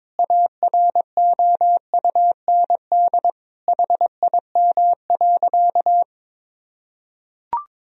Morse code, heard as a keyed tone, about 22 wpm, 700 Hz. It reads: AROUND HIM.
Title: Sentences from Top 1000 Words - Review Part 3 - 22wpm